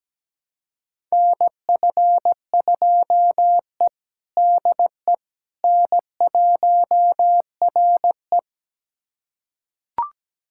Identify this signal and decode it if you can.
Morse code, heard as a keyed tone, about 17 wpm, 700 Hz. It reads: NF2E DE N1RE
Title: USA Callsign DE Callsign - 4 Characters - 17wpm